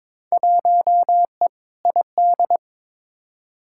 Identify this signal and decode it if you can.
Morse code, heard as a keyed tone, about 21 wpm, 700 Hz. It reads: 1E ID
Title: Field Day Exchanges - 22wpm